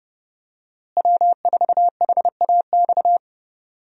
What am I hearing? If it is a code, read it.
Morse code, heard as a keyed tone, about 32 wpm, 700 Hz. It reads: W4HAX
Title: USA Call Signs - 5 Characters - Repeated 3x - 30wpm